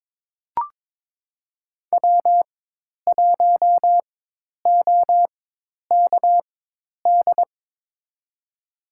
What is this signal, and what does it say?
Morse code, heard as a keyed tone, about 22 wpm, spaced as for 10 wpm, 700 Hz. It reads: W1OKD
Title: USA Call Signs - 5 Characters - 4x Character Spacing - 22wpm